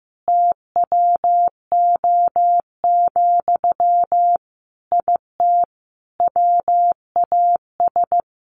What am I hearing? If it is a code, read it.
Morse code, heard as a keyed tone, about 15 wpm, 700 Hz. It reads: TWO, IT WAS
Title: Sentences from Top 1500 Words 15wpm